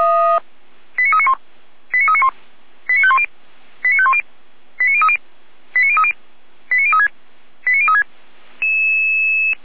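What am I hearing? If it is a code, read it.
Five-tone selective call (ZVEI), heard as quick runs of five tones, tones of about 70 ms. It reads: 89391 89391 89510 89510 89030 89030 89036 89036
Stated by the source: ZVEI five-tone selective call received on an SDR.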